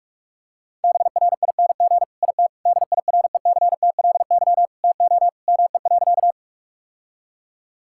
Morse code, heard as a keyed tone, about 41 wpm, 700 Hz. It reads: BRING IT DIRECTLY TO ME.
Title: Sentences from Top 1000 Words - Review Part 5 - 45wpm